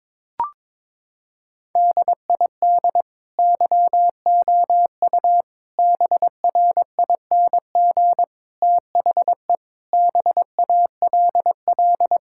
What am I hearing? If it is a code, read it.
Morse code, heard as a keyed tone, about 22 wpm, 700 Hz. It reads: DID YOU BRING THE BALL
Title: Sentences from Top 1000 Words - Review Part 6 - 22wpm